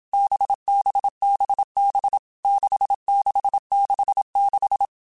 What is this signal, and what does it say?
Morse code, heard as a keyed tone, about 27 wpm, 800 Hz. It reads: BBBB 6666
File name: Basic Letter Groups By7Groups / VB46 / VB46@25